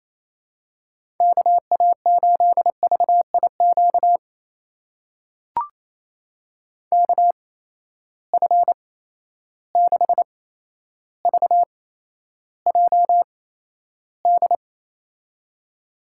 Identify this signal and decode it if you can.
Morse code, heard as a keyed tone, about 28 wpm, spaced as for 12 wpm, 700 Hz. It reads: KA8VIQ KF6VJD
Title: USA Call Signs - 6 Characters - 8x Character Spacing - 28wpm